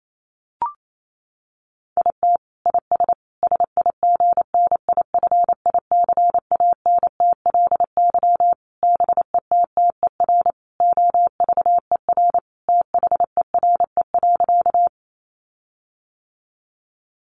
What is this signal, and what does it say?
Morse code, heard as a keyed tone, about 28 wpm, 700 Hz. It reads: IT IS SIGNIFICANTLY BETTER OVER THERE.
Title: Sentences from Top 1500 Words 28wpm